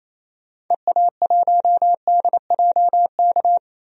Morse code, heard as a keyed tone, about 28 wpm, 700 Hz. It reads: EA1DJK
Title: DXCC-100 Call Signs - 6 Characters - 28wpm